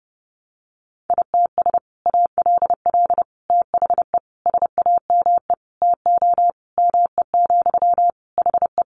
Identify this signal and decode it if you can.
Morse code, heard as a keyed tone, about 30 wpm, 700 Hz. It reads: ITS ALL THE SAME TO ME, HE
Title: Sentences from Top 600 Words 30wpm